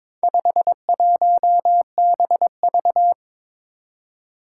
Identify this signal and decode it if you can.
Morse code, heard as a keyed tone, about 22 wpm, 700 Hz. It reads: 51BV
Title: Four Challenging Letter-Number Combinations 22wpm